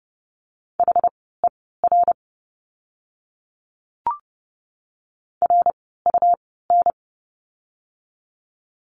Morse code, heard as a keyed tone, about 30 wpm, spaced as for 17 wpm, 700 Hz. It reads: HER RUN
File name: course / 30:10wpm / 115 Lesson 19f - Letter w - Words 30wpm